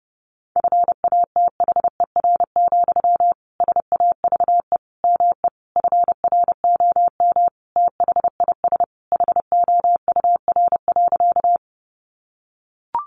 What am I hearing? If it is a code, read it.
Morse code, heard as a keyed tone, about 30 wpm, 700 Hz. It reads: FATHER, SAVE ME FROM THIS HOUR.
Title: Sentences from Top 1100 Words 30wpm